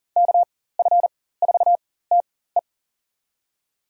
Morse code, heard as a keyed tone, about 40 wpm, spaced as for 19 wpm, 700 Hz. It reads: KF4TE
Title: USA Call Signs - 5 Characters - 4x Character Spacing - 40wpm